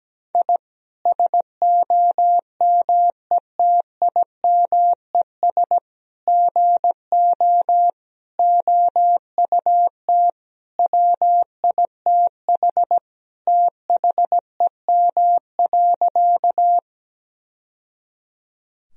Morse code, heard as a keyed tone, about 17 wpm, 700 Hz. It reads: I SOMETIMES GO OUT WITH THEM.
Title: Sentences from Top 1000 Words - Review Part 3 - 17wpm